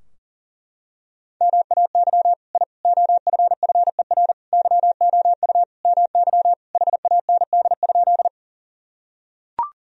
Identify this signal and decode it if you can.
Morse code, heard as a keyed tone, about 40 wpm, 700 Hz. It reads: MAY I OFFER YOU MY HAND?
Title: Sentences from Top 1000 Words - Review Part 4 - 40wpm